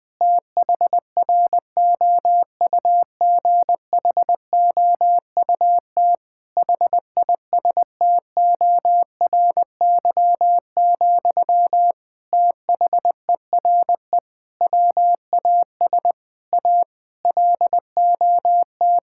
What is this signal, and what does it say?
Morse code, heard as a keyed tone, about 20 wpm, 700 Hz. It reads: THROUGHOUT HISTORY, THERE WAS A LOT
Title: Sentences from Top 1000 Words - Review Part 5 - 20wpm